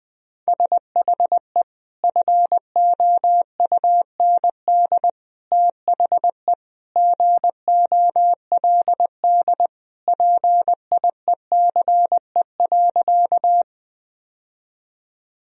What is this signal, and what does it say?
Morse code, heard as a keyed tone, about 20 wpm, 700 Hz. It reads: SHE FOUND THE GOLD PIECE.
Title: Sentences from Top 1000 Words - Review Part 3 - 20wpm